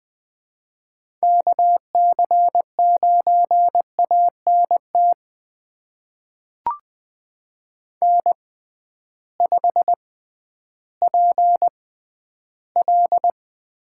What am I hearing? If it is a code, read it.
Morse code, heard as a keyed tone, about 20 wpm, spaced as for 12 wpm, 700 Hz. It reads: KC9ANT N5PL
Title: USA Call Signs - 6x Character Spacing - 20wpm